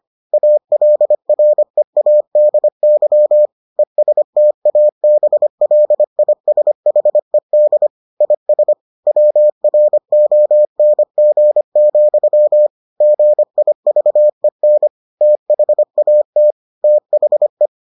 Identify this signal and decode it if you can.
Morse code, heard as a keyed tone, about 25 wpm, 600 Hz. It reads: ALREADY ESTABLISHED IS WRONG, GIVEN THAT THE